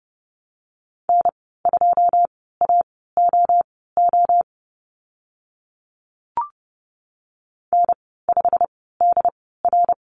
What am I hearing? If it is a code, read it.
Morse code, heard as a keyed tone, about 30 wpm, spaced as for 17 wpm, 700 Hz. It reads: N2AOO N5DR
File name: Lesson 15g - Number 5 - Call Signs 30wpm